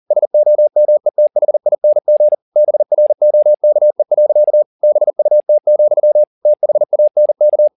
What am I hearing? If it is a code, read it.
Morse code, heard as a keyed tone, about 40 wpm, 600 Hz. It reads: SOMETHING BROKE. BUT, THANK